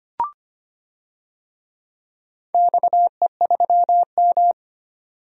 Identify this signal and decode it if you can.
Morse code, heard as a keyed tone, about 25 wpm, 700 Hz. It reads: XE3M
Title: DXCC-100 Call Signs - 4 Characters - 25wpm